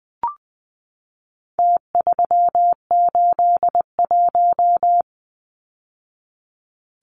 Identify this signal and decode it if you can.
Morse code, heard as a keyed tone, about 20 wpm, 700 Hz. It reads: T381
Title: Numbers - Contest Style - 4-Digit 20wpm